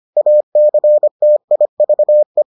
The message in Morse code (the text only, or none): ACTIVE